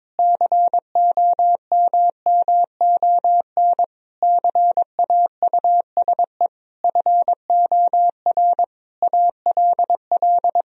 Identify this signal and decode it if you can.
Morse code, heard as a keyed tone, about 22 wpm, 700 Hz. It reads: COMMON CAUSE FOR ALL
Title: Sentences from Top 1000 Words - Review Part 3 - 22wpm